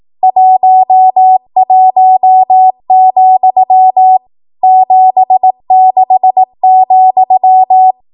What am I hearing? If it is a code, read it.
Morse code, heard as a keyed tone, about 18 wpm, 750 Hz. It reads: 11, 76,
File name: Archive / 18 WPM / 18WPM